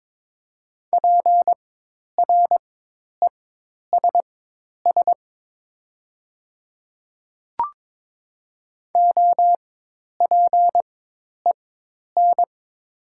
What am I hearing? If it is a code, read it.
Morse code, heard as a keyed tone, about 22 wpm, spaced as for 10 wpm, 700 Hz. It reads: PRESS OPEN
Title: 4x Character Spacing - Top 300 Words - 22wpm